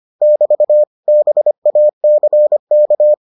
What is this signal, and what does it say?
Morse code, heard as a keyed tone, about 25 wpm, 600 Hz. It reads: <BT> BACK